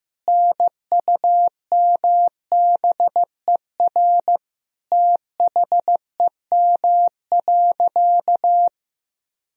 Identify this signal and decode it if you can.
Morse code, heard as a keyed tone, about 15 wpm, 700 Hz. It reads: NUMBER THEM.
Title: Sentences from Top 1000 Words - Review Part 1 - 15wpm